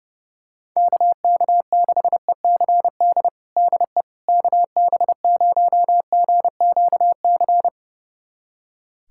Morse code, heard as a keyed tone, about 30 wpm, 700 Hz. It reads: KK6ECD DE KB0GQC
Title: USA Callsign DE Callsign - 6 Characters - 30wpm